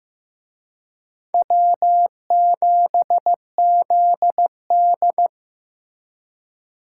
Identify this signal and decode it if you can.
Morse code, heard as a keyed tone, about 18 wpm, 700 Hz. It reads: W7ZD
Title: Speed Racing - USA Call Signs 15wpm